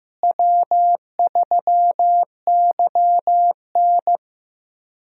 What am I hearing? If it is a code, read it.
Morse code, heard as a keyed tone, about 15 wpm, 700 Hz. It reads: W3YN
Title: USA Call Signs - 4 Characters - Repeated 3x - 15wpm